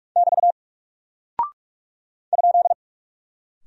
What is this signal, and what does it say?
Morse code, heard as a keyed tone, about 44 wpm, 700 Hz. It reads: <BT> ?
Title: Single Character (Letter:Number:Punctuation:Prosign) Alt 45wpm